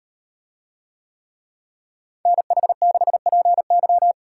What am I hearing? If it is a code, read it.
Morse code, heard as a keyed tone, about 37 wpm, 700 Hz. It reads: NH6PY